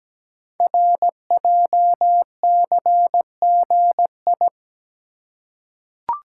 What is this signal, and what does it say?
Morse code, heard as a keyed tone, about 17 wpm, 700 Hz. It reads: RJCGI
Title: Code Groups - 5 Letters - 17wpm